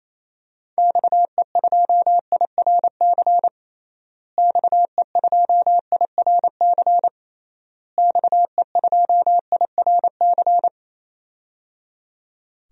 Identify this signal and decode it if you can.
Morse code, heard as a keyed tone, about 28 wpm, 700 Hz. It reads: XE2IRC XE2IRC XE2IRC
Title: DXCC-100 Call Signs - 6 Characters - Repeated 3x - 28wpm